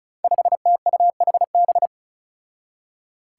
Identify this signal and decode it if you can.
Morse code, heard as a keyed tone, about 36 wpm, 700 Hz. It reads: 5TUHB
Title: Five Challenging Letter-Number Combinations 35wpm